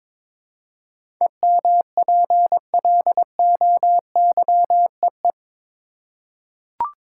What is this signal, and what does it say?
Morse code, heard as a keyed tone, about 21 wpm, 700 Hz. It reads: EMPLOYEE
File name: practice / 22wpm / Eight Letter Words - 22wpm